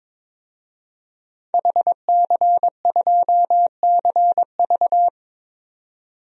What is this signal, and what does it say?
Morse code, heard as a keyed tone, about 22 wpm, 700 Hz. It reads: HC2CV
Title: DXCC-100 Call Signs - 5 Characters - 22wpm